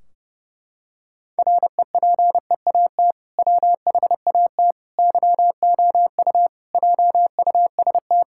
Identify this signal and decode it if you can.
Morse code, heard as a keyed tone, about 30 wpm, 700 Hz. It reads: REPEAT WHAT YOU JUST
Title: Sentences from Top 1600 Words 30wpm